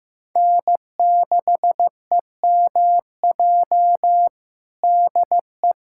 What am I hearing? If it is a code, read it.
Morse code, heard as a keyed tone, about 15 wpm, 700 Hz. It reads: N6EMJ DE
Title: USA Callsign DE Callsign 15wpm